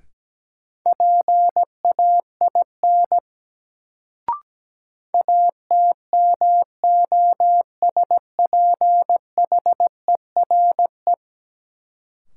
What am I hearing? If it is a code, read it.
Morse code, heard as a keyed tone, about 17 wpm, 700 Hz. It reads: PAIN ATMOSPHERE
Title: Sentences from Top 1600 Words - New Word Drill Alt - 17wpm